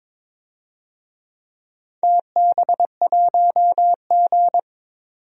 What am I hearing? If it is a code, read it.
Morse code, heard as a keyed tone, about 22 wpm, 700 Hz. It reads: TB1G
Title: DXCC-100 Call Signs - 4 Characters - 22wpm